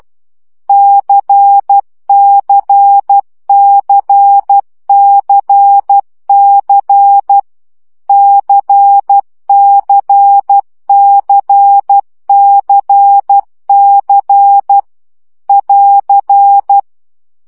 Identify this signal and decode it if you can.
Morse code, heard as a keyed tone, about 12 wpm, 800 Hz. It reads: CCCCC CCCCC <AR>
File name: CW gyakorlas c 10 baud